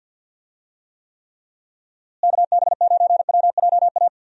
Morse code, heard as a keyed tone, about 52 wpm, 700 Hz. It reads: KB9WJA
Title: USA Call Signs - 6 Characters - Repeated 3x - 50wpm